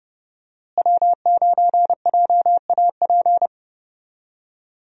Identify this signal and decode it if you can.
Morse code, heard as a keyed tone, about 30 wpm, 700 Hz. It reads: W9JAP
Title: USA Call Signs - 5 Characters - Repeated 3x - 30wpm